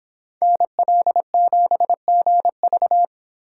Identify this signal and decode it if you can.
Morse code, heard as a keyed tone, about 30 wpm, 700 Hz. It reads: NL7GV